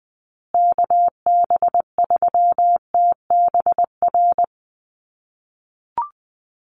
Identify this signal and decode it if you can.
Morse code, heard as a keyed tone, about 20 wpm, 700 Hz. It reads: KB3TBR
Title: USA Call Signs Repeated 3x 20wpm